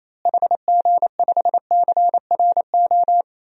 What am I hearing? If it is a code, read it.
Morse code, heard as a keyed tone, about 28 wpm, 700 Hz. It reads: HG5CRO
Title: DXCC-100 Call Signs - 6 Characters - Repeated 3x - 28wpm